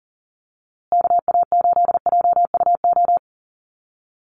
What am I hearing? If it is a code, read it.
Morse code, heard as a keyed tone, about 40 wpm, 700 Hz. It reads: KA8JUO
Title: USA Call Signs Repeated 2x 40wpm